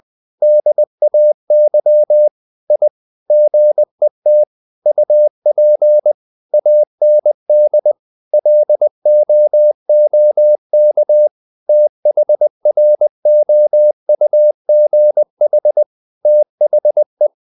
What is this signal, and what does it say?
Morse code, heard as a keyed tone, about 20 wpm, 600 Hz. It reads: DAY I GET UP AND LOOK THROUGH THE
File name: CW / Practice / EnglishQuotations0320